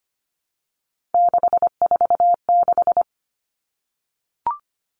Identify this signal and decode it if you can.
Morse code, heard as a keyed tone, about 25 wpm, 700 Hz. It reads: 646
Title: Three Challenging Letter-Number Combinations 25wpm